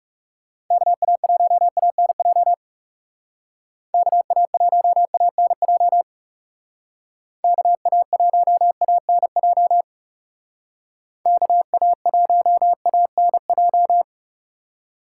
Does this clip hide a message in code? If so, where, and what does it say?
Yes: Morse code from 0.7 to 14.0 s, heard as a keyed tone, about 39 wpm, 700 Hz. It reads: KA1ANJ KA1ANJ KA1ANJ KA1ANJ